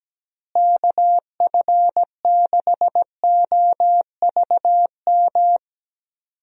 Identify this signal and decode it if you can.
Morse code, heard as a keyed tone, about 17 wpm, 700 Hz. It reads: KF6OVM
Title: USA Call Signs - 6 Characters - Repeated 2x - 17wpm